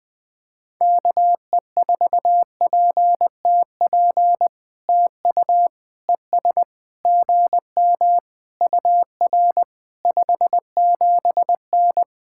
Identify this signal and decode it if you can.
Morse code, heard as a keyed tone, about 20 wpm, 700 Hz. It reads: KE4PTP TU ES GM UR 57N
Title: POTA QSO Practice - 20wpm